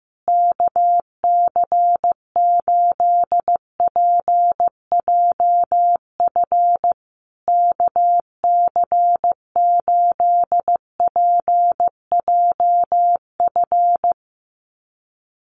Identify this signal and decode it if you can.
Morse code, heard as a keyed tone, about 15 wpm, 700 Hz. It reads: KC8PJF KC8PJF
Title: USA Call Signs Repeated 3x 15wpm